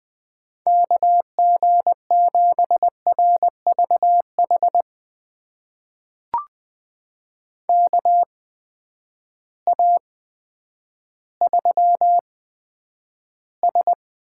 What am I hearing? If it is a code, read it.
Morse code, heard as a keyed tone, about 20 wpm, spaced as for 10 wpm, 700 Hz. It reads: KG7RVH KA3S
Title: USA Call Signs - 6 Characters - 8x Character Spacing - 20wpm